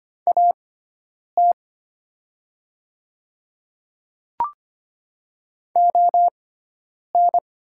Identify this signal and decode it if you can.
Morse code, heard as a keyed tone, about 25 wpm, spaced as for 9 wpm, 700 Hz. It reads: AT ON
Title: 6x Character Spacing - Top 100 Words - 25wpm